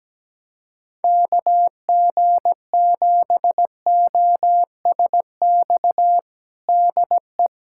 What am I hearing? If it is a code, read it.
Morse code, heard as a keyed tone, about 17 wpm, 700 Hz. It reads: KG7OSX DE
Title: USA Callsign DE Callsign 17wpm